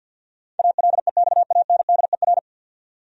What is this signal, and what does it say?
Morse code, heard as a keyed tone, about 52 wpm, 700 Hz. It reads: ALEXANDER
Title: USA Common Names - 50wpm